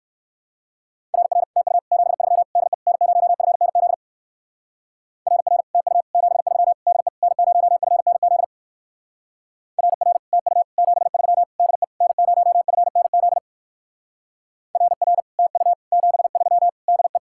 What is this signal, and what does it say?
Morse code, heard as a keyed tone, about 66 wpm, 700 Hz. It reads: RR TU 73 DE N0FNZ RR TU 73 DE N0FNZ RR TU 73 DE N0FNZ RR TU 73 DE